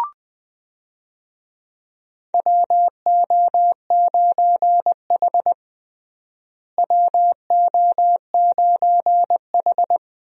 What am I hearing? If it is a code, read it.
Morse code, heard as a keyed tone, about 20 wpm, 700 Hz. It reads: WO9H WO9H
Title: USA Call Signs - 4 Characters - Repeated 3x - 20wpm